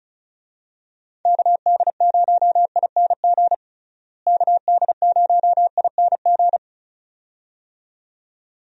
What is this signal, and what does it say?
Morse code, heard as a keyed tone, about 36 wpm, 700 Hz. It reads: KD0ING KD0ING
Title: USA Call Signs - 6 Characters - Repeated 2x - 35wpm